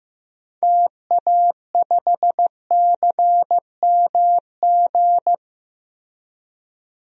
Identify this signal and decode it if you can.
Morse code, heard as a keyed tone, about 15 wpm, 700 Hz. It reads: TA5CMG
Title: DXCC-100 Call Signs - 6 Characters - Repeated 2x - 15wpm